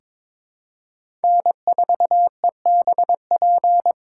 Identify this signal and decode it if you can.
Morse code, heard as a keyed tone, about 22 wpm, 700 Hz. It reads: N4EBP
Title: USA Call Signs - 5 Characters - 3x Character Spacing - 22wpm